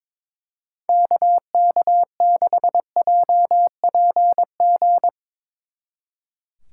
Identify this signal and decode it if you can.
Morse code, heard as a keyed tone, about 24 wpm, 690 Hz. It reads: KK6JPG
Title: Speed Racing - USA Call Signs 22wpm